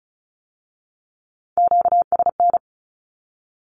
Morse code, heard as a keyed tone, about 36 wpm, 700 Hz. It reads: QSN
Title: Q-Codes with Spoken Meaning - 35wpm